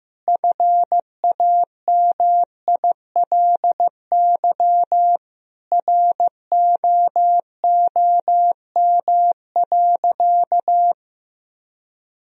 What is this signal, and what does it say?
Morse code, heard as a keyed tone, about 16 wpm, 700 Hz. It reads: FAMILY ROOM.
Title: Sentences from Top 1000 Words - Review Part 3 - 15wpm